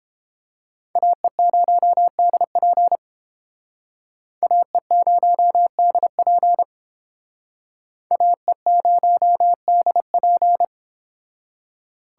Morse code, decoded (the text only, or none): AE0DP AE0DP AE0DP